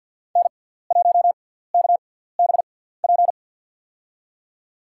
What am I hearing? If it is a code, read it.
Morse code, heard as a keyed tone, about 52 wpm, spaced as for 17 wpm, 690 Hz. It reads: N1KBP
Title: USA Call Signs - 5 Characters - 6x Character Spacing - 50wpm